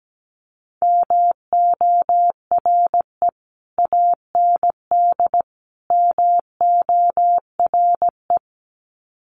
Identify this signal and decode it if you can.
Morse code, heard as a keyed tone, about 17 wpm, 700 Hz. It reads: MORE AND MORE
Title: Sets of 3 Words 17wpm